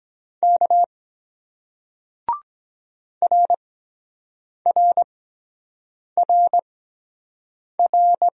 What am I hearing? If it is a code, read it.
Morse code, heard as a keyed tone, about 22 wpm, spaced as for 7 wpm, 700 Hz. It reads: K RRRR